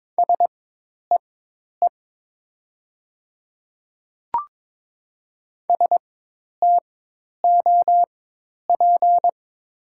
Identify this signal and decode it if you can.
Morse code, heard as a keyed tone, about 22 wpm, spaced as for 10 wpm, 690 Hz. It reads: SEE STOP
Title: 4x Character Spacing - Top 500 Words - 22wpm